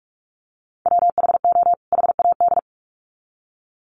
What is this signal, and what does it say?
Morse code, heard as a keyed tone, about 44 wpm, 700 Hz. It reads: WHO HAD
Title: Sets of 2 Words 45wpm